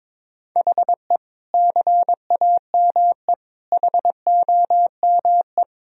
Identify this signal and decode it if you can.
Morse code, heard as a keyed tone, about 22 wpm, 700 Hz. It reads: HE CAME HOME
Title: Sentences from Top 1000 Words - Review Part 2 - 22wpm